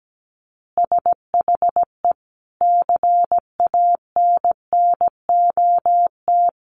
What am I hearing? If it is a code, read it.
Morse code, heard as a keyed tone, about 17 wpm, 700 Hz. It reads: SHE CANNOT
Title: Additional Easy Phrases 17wpm